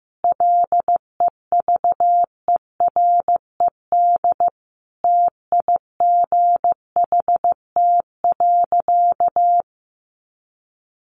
Morse code, heard as a keyed tone, about 15 wpm, 700 Hz. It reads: LEVERED TIGHT.
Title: A Sound Of Thunder 15wpm